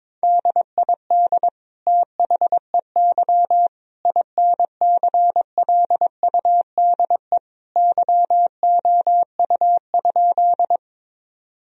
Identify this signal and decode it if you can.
Morse code, heard as a keyed tone, about 23 wpm, 700 Hz. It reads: DID THEY INCLUDE YOU?
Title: Sentences from Top 1000 Words - Review Part 3 - 22wpm